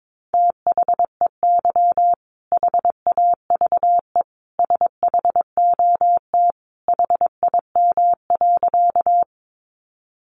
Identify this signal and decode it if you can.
Morse code, heard as a keyed tone, about 22 wpm, 700 Hz. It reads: THEY HAVE SHOT HIM.
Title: Sentences from Top 1100 Words 22wpm